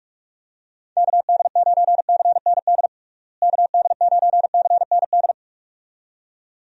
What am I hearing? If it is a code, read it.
Morse code, heard as a keyed tone, about 46 wpm, 700 Hz. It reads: KD9CND KD9CND
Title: USA Call Signs - 6 Characters - Repeated 2x - 45wpm